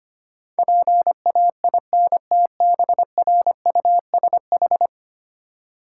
Morse code, heard as a keyed tone, about 25 wpm, 700 Hz. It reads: PAINTBRUSH
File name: practice / 25wpm / Easy Words with Hints - 25wpm